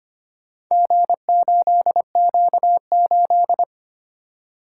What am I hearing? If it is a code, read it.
Morse code, heard as a keyed tone, about 25 wpm, 700 Hz. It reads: G8Q8
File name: practice / 25wpm / Four Letter-Number Combinations - 25wpm